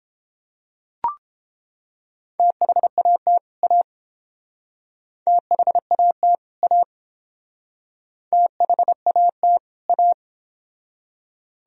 Morse code, heard as a keyed tone, about 30 wpm, 700 Hz. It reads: THAT A THAT A THAT A